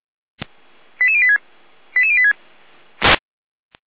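Five-tone selective call (ZVEI), heard as quick runs of five tones, tones of about 70 ms. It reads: 88086 88086
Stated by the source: ZVEI five-tone call received by SDR.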